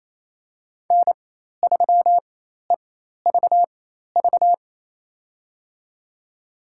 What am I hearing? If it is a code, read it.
Morse code, heard as a keyed tone, about 28 wpm, spaced as for 13 wpm, 700 Hz. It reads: N3EVV
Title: USA Call Signs - 5 Characters - 4x Character Spacing - 28wpm